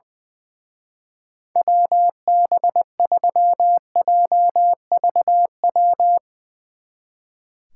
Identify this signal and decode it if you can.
Morse code, heard as a keyed tone, about 21 wpm, 700 Hz. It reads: WB3JVW
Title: Speed Racing - USA Call Signs 20wpm